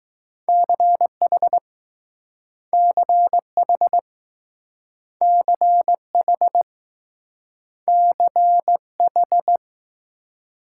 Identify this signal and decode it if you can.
Morse code, heard as a keyed tone, about 20 wpm, 700 Hz. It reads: CH CH CH CH